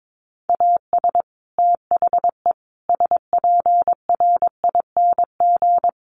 Morse code, heard as a keyed tone, about 22 wpm, 700 Hz. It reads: AS THE SPRING
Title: Sentences from Top 1200 Words 22wpm